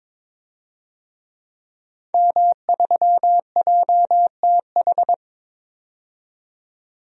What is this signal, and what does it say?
Morse code, heard as a keyed tone, about 22 wpm, 700 Hz. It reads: M3JTH
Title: DXCC-100 Call Signs - 5 Characters - 22wpm